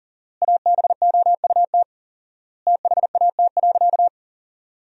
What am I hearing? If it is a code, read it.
Morse code, heard as a keyed tone, about 40 wpm, 700 Hz. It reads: ABOUT THAT.
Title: 4x Word-Spacing - Sentences from Top 400 Words - 40wpm